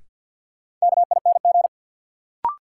Morse code, heard as a keyed tone, about 52 wpm, 700 Hz. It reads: KING
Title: Sentences from Top 400 Words - New Word Drill - 50wpm